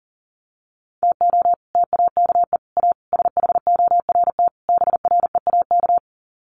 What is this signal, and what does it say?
Morse code, heard as a keyed tone, about 40 wpm, 700 Hz. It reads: TO TAKE A SHORT BREAK
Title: Sets of 5 Words Encore2 40wpm